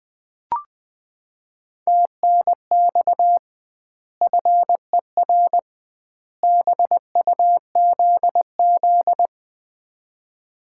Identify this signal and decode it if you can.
Morse code, heard as a keyed tone, about 20 wpm, 700 Hz. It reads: TNX FER BUZZ
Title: Common QSO Phrases - 2x Word-Spacing - 20wpm